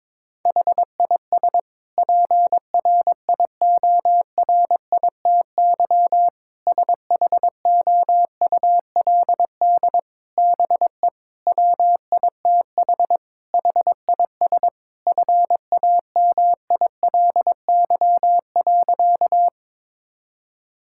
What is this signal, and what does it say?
Morse code, heard as a keyed tone, about 22 wpm, 700 Hz. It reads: HIS PRIORITY SHOULD BE WITH HIS FAMILY.
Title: Sentences from Top 1600 Words 22wpm